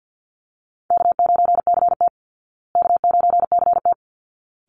Morse code, heard as a keyed tone, about 48 wpm, 700 Hz. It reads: K9CT K9CT
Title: Call Signs Repeated 3x - 50wpm